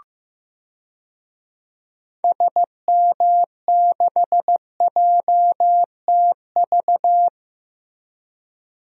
Morse code, heard as a keyed tone, about 15 wpm, 700 Hz. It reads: SM6JTV
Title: DXCC-100 Call Signs - 6 Characters - Repeated 2x - 15wpm